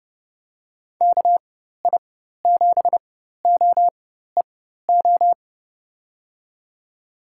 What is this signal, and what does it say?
Morse code, heard as a keyed tone, about 30 wpm, spaced as for 14 wpm, 700 Hz. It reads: KI7OEO
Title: USA Call Signs - 6 Characters - 4x Character Spacing - 30wpm